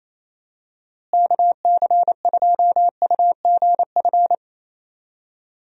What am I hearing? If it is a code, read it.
Morse code, heard as a keyed tone, about 27 wpm, 700 Hz. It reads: KC2UGF
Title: USA Call Signs - 6 Characters - Repeated 2x - 28wpm